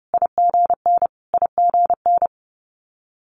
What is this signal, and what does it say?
Morse code, heard as a keyed tone, about 30 wpm, 700 Hz. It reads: IGN IGN
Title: Multi-letter Phonemes Repeated 3x 30wpm